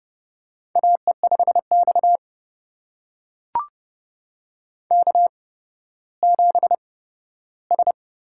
Morse code, heard as a keyed tone, about 30 wpm, spaced as for 15 wpm, 700 Hz. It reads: AE5X K7S
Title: USA Call Signs - 8x Character Spacing - 30wpm